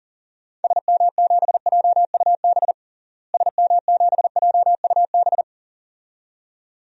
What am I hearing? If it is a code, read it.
Morse code, heard as a keyed tone, about 40 wpm, 700 Hz. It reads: SM7JUB SM7JUB
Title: DXCC-100 Call Signs - 6 Characters - Repeated 2x - 40wpm